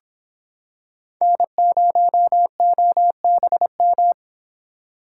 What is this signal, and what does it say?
Morse code, heard as a keyed tone, about 25 wpm, 700 Hz. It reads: N0OBM